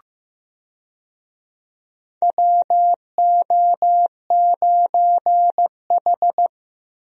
Morse code, heard as a keyed tone, about 15 wpm, 700 Hz. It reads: WO9H
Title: USA Call Signs - 4 Characters - Repeated 3x - 15wpm